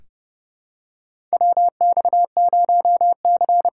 Morse code, heard as a keyed tone, about 30 wpm, 700 Hz. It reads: WX0C
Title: USA Call Signs - 6 Characters - 8x Character Spacing - 30wpm